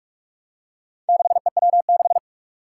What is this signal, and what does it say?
Morse code, heard as a keyed tone, about 43 wpm, 700 Hz. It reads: 6EW6
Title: Four Challenging Letter-Number Combinations 45wpm